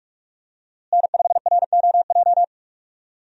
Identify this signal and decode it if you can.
Morse code, heard as a keyed tone, about 44 wpm, 700 Hz. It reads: N5ROJ
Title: USA Call Signs - 5 Characters - Repeated 2x - 45wpm